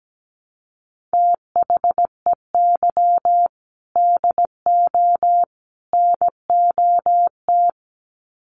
Morse code, heard as a keyed tone, about 17 wpm, 700 Hz. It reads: THEY DO NOT
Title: Sets of 3 Words 17wpm